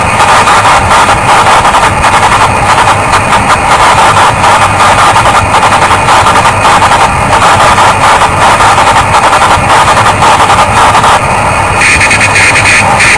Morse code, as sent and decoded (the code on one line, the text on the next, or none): none